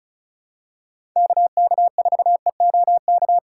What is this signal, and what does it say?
Morse code, heard as a keyed tone, about 36 wpm, 690 Hz. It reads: KK4EOK
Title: USA Call Signs - 3x Character Spacing - 35wpm